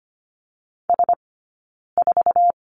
Morse code, heard as a keyed tone, about 25 wpm, spaced as for 9 wpm, 690 Hz. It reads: S4
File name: course / 25:10wpm / 092 Lesson 16d - Letter c - Two Characters 25wpm